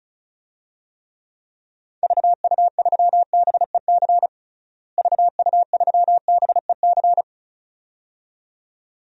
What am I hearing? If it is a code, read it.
Morse code, heard as a keyed tone, about 34 wpm, 700 Hz. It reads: VU3BEC VU3BEC
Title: DXCC-100 Call Signs - 6 Characters - Repeated 2x - 35wpm